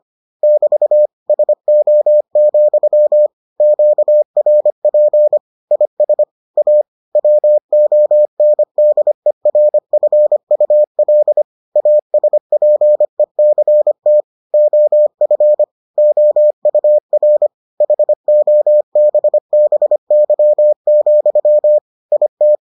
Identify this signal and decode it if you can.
Morse code, heard as a keyed tone, about 25 wpm, 600 Hz. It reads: <BT> SO, QRP IS A WONDERFUL ASPECT OF OUR HOBBY, IT